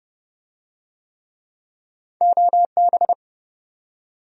Morse code, heard as a keyed tone, about 27 wpm, 700 Hz. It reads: OB